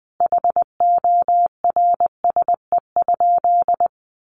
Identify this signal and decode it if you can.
Morse code, heard as a keyed tone, about 20 wpm, 700 Hz. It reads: HORSE?
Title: Sentences from Top 1400 Words 20wpm